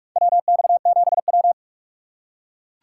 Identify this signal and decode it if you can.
Morse code, heard as a keyed tone, about 44 wpm, 700 Hz. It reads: WX7W
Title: USA Call Signs - 4 Characters - 6x Character Spacing - 45wpm